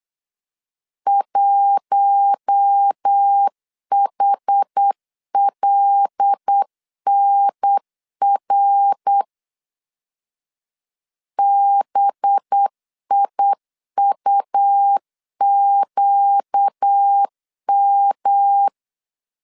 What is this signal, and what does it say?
Morse code, heard as a keyed tone, about 8 wpm, 800 Hz. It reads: 1HLNR BIUQM
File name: CW / DET / 1 6 1500C S B 8wpm 5142